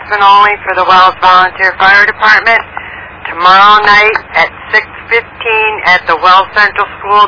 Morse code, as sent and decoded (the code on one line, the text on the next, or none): none